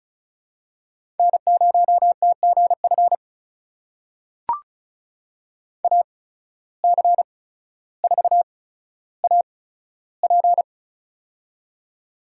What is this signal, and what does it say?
Morse code, heard as a keyed tone, about 36 wpm, spaced as for 15 wpm, 700 Hz. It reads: N0TGF AC4AP
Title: USA Call Signs - 5 Characters - 8x Character Spacing - 35wpm